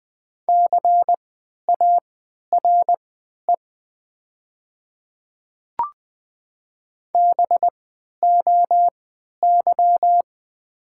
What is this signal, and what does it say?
Morse code, heard as a keyed tone, about 20 wpm, spaced as for 11 wpm, 700 Hz. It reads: CARE BOY
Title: 3x Character Spacing - Top 500 Words - 20wpm